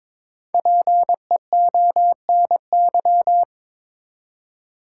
Morse code, heard as a keyed tone, about 22 wpm, 700 Hz. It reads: PEONY
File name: Difficult Words with Hints - 22wpm